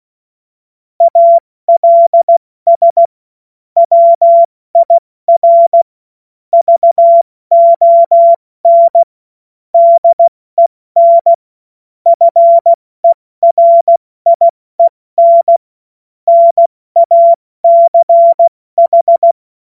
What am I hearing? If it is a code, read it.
Morse code, heard as a keyed tone, about 16 wpm, spaced as for 14 wpm, 680 Hz. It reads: ALS WIR VON DEN FERIEN NACH
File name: Morsekurs / Witze / Witz 16 Januar 2024 70 F80